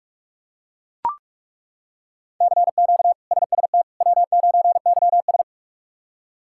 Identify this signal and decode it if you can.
Morse code, heard as a keyed tone, about 41 wpm, 700 Hz. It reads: CQ SST W9YS
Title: K1USN SST QSO Practice - 45wpm